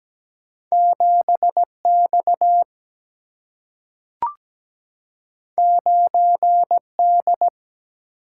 Morse code, heard as a keyed tone, about 17 wpm, 700 Hz. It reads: 7X 9D
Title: Single-Letter Single-Number 17wpm